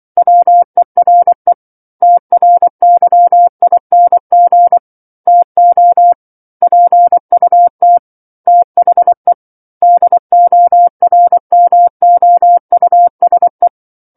Morse code, heard as a keyed tone, about 24 wpm, 700 Hz. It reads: WERE TRYING TO PUT THE DORMOUSE
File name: 07-Chapter-07 24 WPM